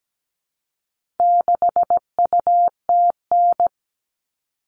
Morse code, heard as a keyed tone, about 17 wpm, 700 Hz. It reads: 6UTN
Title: All Cut Numbers - 4-Digit 17wpm